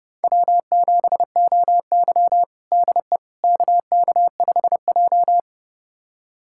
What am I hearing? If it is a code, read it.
Morse code, heard as a keyed tone, about 30 wpm, 700 Hz. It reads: W7OY DE KK5J
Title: USA Callsign DE Callsign - 4 Characters - 30wpm